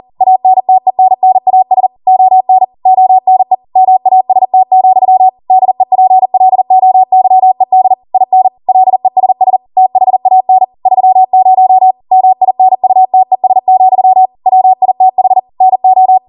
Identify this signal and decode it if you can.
Morse code, heard as a keyed tone, about 40 wpm, 750 Hz. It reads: ANTENNAS ON ONE MAST, DEPLOYED IN LESS THAN 30 MINUTES, WITH NO
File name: Morse / 40WPM